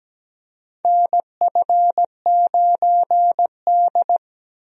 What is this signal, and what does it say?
Morse code, heard as a keyed tone, about 17 wpm, 700 Hz. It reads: NF9D
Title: USA Call Signs - 4 Characters - Repeated 2x - 17wpm